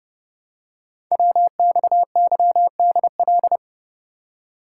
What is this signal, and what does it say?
Morse code, heard as a keyed tone, about 30 wpm, 700 Hz. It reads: WXYDL
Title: Code Groups - 5 Letters - 30wpm